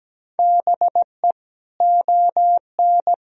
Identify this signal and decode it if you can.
Morse code, heard as a keyed tone, about 17 wpm, 700 Hz. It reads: BE ON
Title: Sentences from Top 1000 Words - Review Part 4 - 17wpm